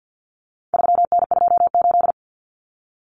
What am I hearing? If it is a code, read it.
Morse code, heard as a keyed tone, about 50 wpm, 700 Hz. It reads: 3N28
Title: Common Cut Numbers - 4-Digit 50wpm